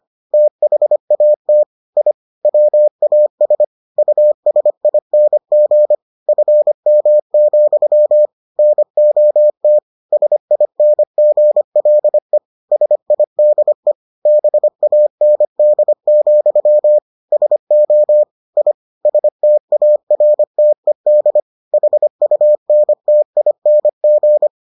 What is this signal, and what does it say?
Morse code, heard as a keyed tone, about 25 wpm, 600 Hz. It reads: THAT I WAS USING FM, NOT SINGLE SIDE BAND, SO I STARTED HUNTING